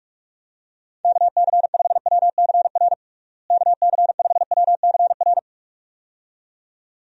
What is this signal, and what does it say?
Morse code, heard as a keyed tone, about 41 wpm, 700 Hz. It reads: KC5WCR KC5WCR
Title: USA Call Signs - 6 Characters - Repeated 2x - 45wpm